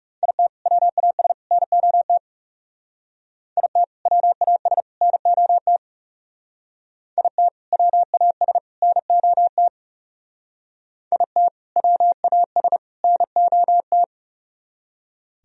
Morse code, heard as a keyed tone, about 37 wpm, 700 Hz. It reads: IT WAS NOT IT WAS NOT IT WAS NOT IT WAS NOT
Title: Speed-Racing - Sets of 3 Words 30wpm